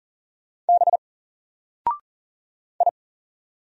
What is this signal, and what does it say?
Morse code, heard as a keyed tone, about 39 wpm, 700 Hz. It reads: B I
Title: Single Character (Letter:Number:Punctuation:Prosign) Alt 40wpm